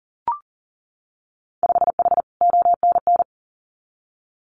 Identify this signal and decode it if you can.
Morse code, heard as a keyed tone, about 40 wpm, 700 Hz. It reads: 5H ONN
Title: Winter Field Day Exchanges Encore 40wpm